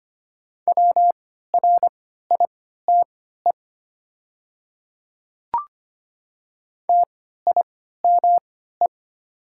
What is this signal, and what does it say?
Morse code, heard as a keyed tone, about 25 wpm, spaced as for 14 wpm, 690 Hz. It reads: WRITE TIME
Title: 3x Character Spacing - Top 200 Words - 25wpm